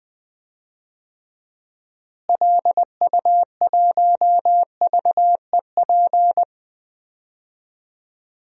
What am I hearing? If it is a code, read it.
Morse code, heard as a keyed tone, about 20 wpm, 700 Hz. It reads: LU1VEP
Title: DXCC-100 Call Signs - 6 Characters - Repeated 2x - 20wpm